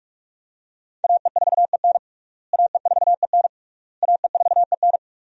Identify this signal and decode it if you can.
Morse code, heard as a keyed tone, about 41 wpm, 700 Hz. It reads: AE4EN AE4EN AE4EN
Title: USA Call Signs - 5 Characters - Repeated 3x - 45wpm